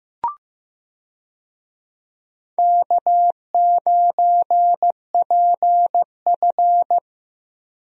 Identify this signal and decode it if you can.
Morse code, heard as a keyed tone, about 15 wpm, 700 Hz. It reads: K9PF
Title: USA Call Signs - 4 Characters - Repeated 3x - 15wpm